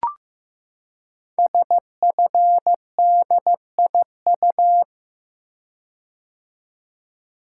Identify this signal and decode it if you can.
Morse code, heard as a keyed tone, about 15 wpm, 700 Hz. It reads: SFDIU
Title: Code Groups - 5 Letters - 15wpm